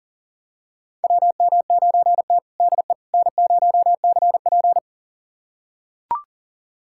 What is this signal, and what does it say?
Morse code, heard as a keyed tone, about 37 wpm, 700 Hz. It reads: WM9T DE N0CP
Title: USA Callsign DE Callsign - 4 Characters - 40wpm